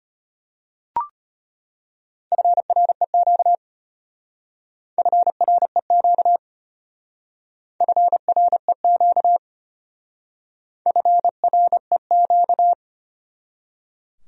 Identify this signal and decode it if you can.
Morse code, heard as a keyed tone, about 33 wpm, 700 Hz. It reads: FREQ FREQ FREQ FREQ